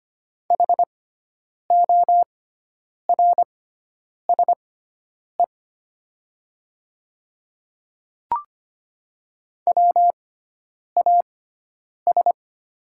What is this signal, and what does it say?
Morse code, heard as a keyed tone, about 25 wpm, spaced as for 9 wpm, 700 Hz. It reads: HORSE WAS
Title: 6x Character Spacing - Top 300 Words - 25wpm